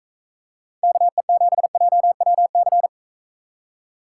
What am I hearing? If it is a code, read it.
Morse code, heard as a keyed tone, about 45 wpm, 700 Hz. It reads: KE7JWC